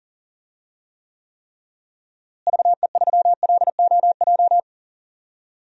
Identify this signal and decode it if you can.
Morse code, heard as a keyed tone, about 40 wpm, 700 Hz. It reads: VE3LOJ
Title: DXCC-100 Call Signs - 6 Characters - Repeated 3x - 40wpm